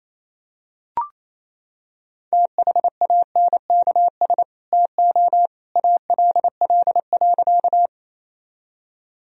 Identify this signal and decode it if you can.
Morse code, heard as a keyed tone, about 27 wpm, 700 Hz. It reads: THANKS TO ALL.
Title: Sentences from Top 1000 Words - Review Part 3 - 28wpm